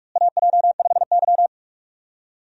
Morse code, heard as a keyed tone, about 46 wpm, 700 Hz. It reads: AJ5Y
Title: USA Call Signs - 4 Characters - Repeated 3x - 45wpm